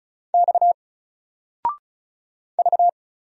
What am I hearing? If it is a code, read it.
Morse code, heard as a keyed tone, about 35 wpm, 700 Hz. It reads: X V
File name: Single-Letter-Number-35wpm